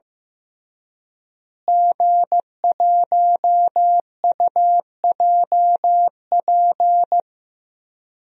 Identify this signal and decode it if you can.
Morse code, heard as a keyed tone, about 15 wpm, 700 Hz. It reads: G1UJP
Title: DXCC-100 Call Signs - 5 Characters - Repeated 3x - 15wpm